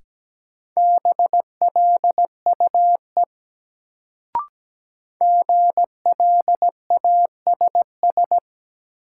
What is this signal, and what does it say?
Morse code, heard as a keyed tone, about 17 wpm, 700 Hz. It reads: BLUE GLASS
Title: Sentences from Top 1000 Words - Review Part 3 - New Word Drill - 17wpm